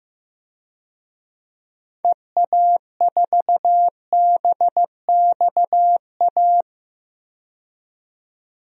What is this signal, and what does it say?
Morse code, heard as a keyed tone, about 15 wpm, 700 Hz. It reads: EA4BXA
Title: DXCC-100 Call Signs - 6 Characters - 15wpm